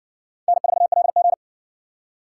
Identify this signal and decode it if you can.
Morse code, heard as a keyed tone, about 58 wpm, 700 Hz. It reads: N4LG